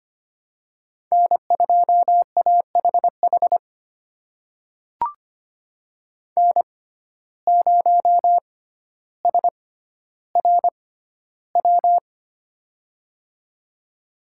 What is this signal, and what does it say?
Morse code, heard as a keyed tone, about 25 wpm, spaced as for 13 wpm, 700 Hz. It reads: N2AHH N0SRW
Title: USA Call Signs - 5 Characters - 6x Character Spacing - 25wpm